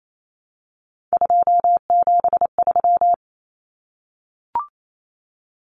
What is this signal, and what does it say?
Morse code, heard as a keyed tone, about 28 wpm, 700 Hz. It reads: 273
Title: Numbers - 3-Digit 28wpm